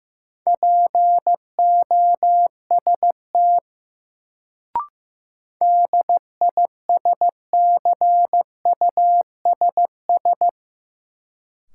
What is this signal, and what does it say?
Morse code, heard as a keyed tone, about 15 wpm, 700 Hz. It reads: POST DISCUSS
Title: Sentences from Top 1000 Words - New Word Drill - 15wpm